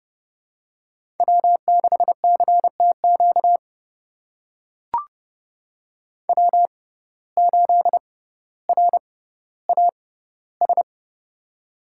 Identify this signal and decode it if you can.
Morse code, heard as a keyed tone, about 30 wpm, spaced as for 15 wpm, 700 Hz. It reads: W6CTQ W8RAS
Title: USA Call Signs - 5 Characters - 6x Character Spacing - 30wpm